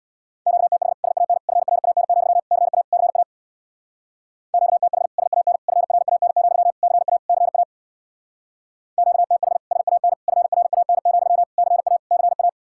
Morse code, heard as a keyed tone, about 60 wpm, 700 Hz. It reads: QTH SAN FRAN, CA CA QTH SAN FRAN, CA CA QTH SAN FRAN, CA CA
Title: Speed-Racing Common QSO Phrases - 50wpm